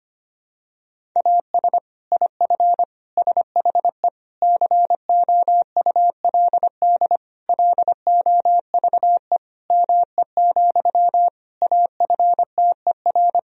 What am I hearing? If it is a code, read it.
Morse code, heard as a keyed tone, about 25 wpm, 700 Hz. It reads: AS IF SHE COULD LOVE ME, AFTER
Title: Sentences from Top 1000 Words - Review Part 1 - 25wpm